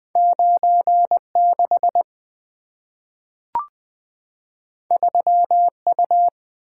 Morse code, heard as a keyed tone, about 20 wpm, 700 Hz. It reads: 96 3U
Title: Two Letter-Number Combinations - 20wpm